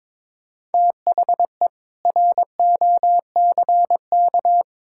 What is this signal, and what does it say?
Morse code, heard as a keyed tone, about 22 wpm, 700 Hz. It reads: THE ROCK
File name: practice / 22wpm / Sentences with 4-Word Phrases - 22wpm